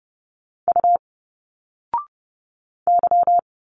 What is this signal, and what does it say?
Morse code, heard as a keyed tone, about 30 wpm, 700 Hz. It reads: U Y
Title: Single Letters 30wpm